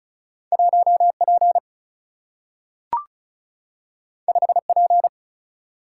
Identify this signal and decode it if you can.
Morse code, heard as a keyed tone, about 36 wpm, 700 Hz. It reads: 1 P 5 P
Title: Single-Letter Single-Number 35wpm